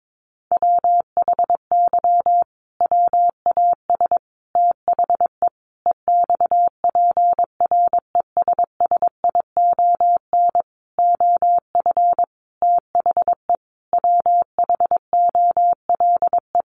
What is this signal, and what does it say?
Morse code, heard as a keyed tone, about 22 wpm, 700 Hz. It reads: WHY WAS THE EXPRESSION OF THE WHOLE
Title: Sentences from Top 1400 Words 22wpm